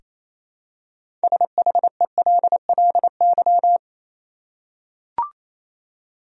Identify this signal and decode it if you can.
Morse code, heard as a keyed tone, about 27 wpm, 700 Hz. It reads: SHELLY
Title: USA Common Names - 28wpm